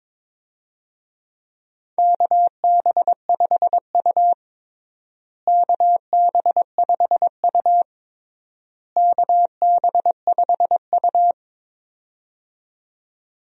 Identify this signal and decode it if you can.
Morse code, heard as a keyed tone, about 22 wpm, 700 Hz. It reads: KB5U KB5U KB5U
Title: USA Call Signs - 4 Characters - Repeated 3x - 22wpm